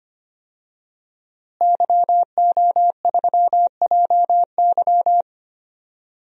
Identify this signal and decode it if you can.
Morse code, heard as a keyed tone, about 25 wpm, 700 Hz. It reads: YO3JY
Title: DXCC-100 Call Signs - 5 Characters - Repeated 3x - 25wpm